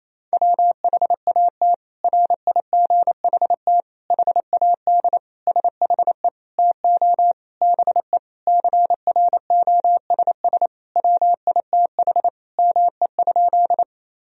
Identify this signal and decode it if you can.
Morse code, heard as a keyed tone, about 28 wpm, 700 Hz. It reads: WHAT RIGHT HAD SHE TO BE CROSS WITH ME?
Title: Sentences from Top 1000 Words - Review Part 1 - 28wpm